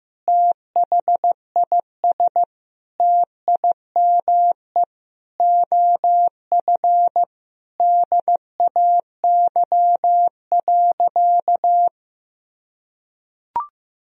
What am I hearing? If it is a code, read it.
Morse code, heard as a keyed tone, about 15 wpm, 700 Hz. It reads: THIS TIME OF DAY.
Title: Sentences from Top 1000 Words - Review Part 4 - 15wpm